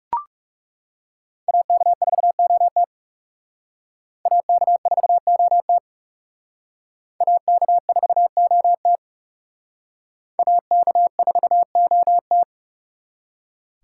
Morse code, heard as a keyed tone, about 38 wpm, 700 Hz. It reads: AK4OT AK4OT AK4OT AK4OT